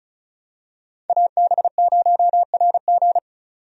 Morse code, heard as a keyed tone, about 34 wpm, 700 Hz. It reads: AB0RG
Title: USA Call Signs - 5 Characters - 35wpm